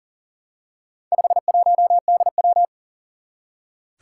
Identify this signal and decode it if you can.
Morse code, heard as a keyed tone, about 38 wpm, 700 Hz. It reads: 51DW
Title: Four Challenging Letter-Number Combinations 40wpm